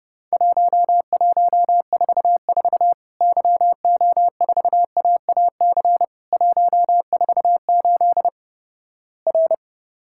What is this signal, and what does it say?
Morse code, heard as a keyed tone, about 30 wpm, 700 Hz. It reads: 1144 YO4AAC 148 R
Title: Worked All Europe DX Contest - QTC Exchange - 30wpm